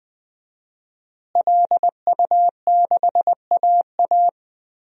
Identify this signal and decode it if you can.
Morse code, heard as a keyed tone, about 20 wpm, 700 Hz. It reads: LU6AA
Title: DXCC-100 Call Signs - 5 Characters - Repeated 3x - 20wpm